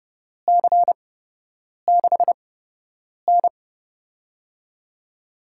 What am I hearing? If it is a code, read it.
Morse code, heard as a keyed tone, about 30 wpm, spaced as for 8 wpm, 700 Hz. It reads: C6N
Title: USA Call Signs - 4 Characters - 8x Character Spacing - 30wpm